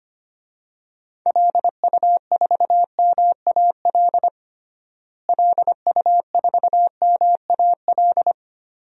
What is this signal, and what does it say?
Morse code, heard as a keyed tone, about 25 wpm, 700 Hz. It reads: LU4MAL LU4MAL
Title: DXCC-100 Call Signs - 6 Characters - Repeated 3x - 25wpm